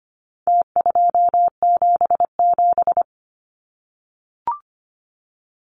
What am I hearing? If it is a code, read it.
Morse code, heard as a keyed tone, about 25 wpm, 700 Hz. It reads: T277
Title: Numbers - Contest Style - 4-Digit 25wpm